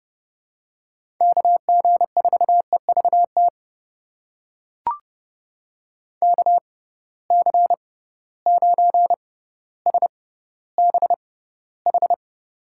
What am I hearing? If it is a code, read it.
Morse code, heard as a keyed tone, about 30 wpm, spaced as for 15 wpm, 700 Hz. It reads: KG4EVT KC9SBH
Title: USA Call Signs - 6 Characters - 6x Character Spacing - 30wpm